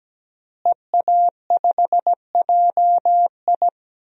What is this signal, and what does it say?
Morse code, heard as a keyed tone, about 17 wpm, 700 Hz. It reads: EA5JI
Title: DXCC-100 Call Signs - 5 Characters - Repeated 3x - 17wpm